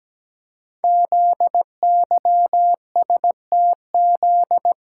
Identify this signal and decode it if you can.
Morse code, heard as a keyed tone, about 17 wpm, 700 Hz. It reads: ZYSTZ
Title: Code Groups - 5 Letters - 17wpm